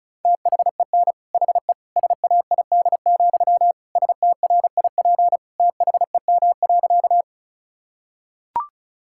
Morse code, heard as a keyed tone, about 34 wpm, 700 Hz. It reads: THEN HE SAID, STRIP THEM.
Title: Sentences from Top 1600 Words 35wpm